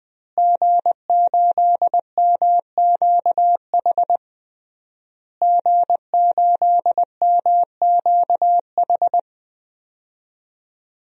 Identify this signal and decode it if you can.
Morse code, heard as a keyed tone, about 20 wpm, 700 Hz. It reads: G8MQH G8MQH
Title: DXCC-100 Call Signs - 5 Characters - Repeated 3x - 20wpm